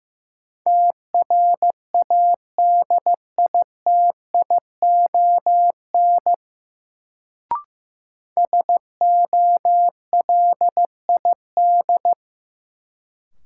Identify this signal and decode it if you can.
Morse code, heard as a keyed tone, about 15 wpm, 700 Hz. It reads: TRADITION SOLID
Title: Sentences from Top 1500 Words - New Word Drill Alt - 15wpm